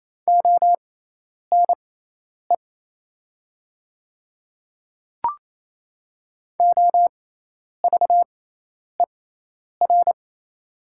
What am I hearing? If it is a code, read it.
Morse code, heard as a keyed tone, about 28 wpm, spaced as for 10 wpm, 700 Hz. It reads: ONE OVER
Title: 6x Character Spacing - Top 300 Words - 28wpm